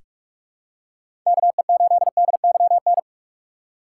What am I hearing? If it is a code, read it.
Morse code, heard as a keyed tone, about 43 wpm, 700 Hz. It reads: KE8DYN
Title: USA Call Signs - 4x Character Spacing - 45wpm